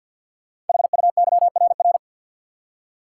Morse code, heard as a keyed tone, about 48 wpm, 690 Hz. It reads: HUYRR
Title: Code Groups - 5 Letters - 50wpm